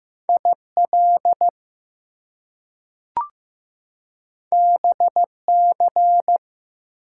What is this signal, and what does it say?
Morse code, heard as a keyed tone, about 15 wpm, 700 Hz. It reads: IL BC
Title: US States and Canadian Provinces 15wpm